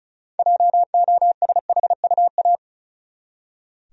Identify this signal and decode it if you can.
Morse code, heard as a keyed tone, about 36 wpm, 700 Hz. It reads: JOSHUA
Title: USA Common Names - 35wpm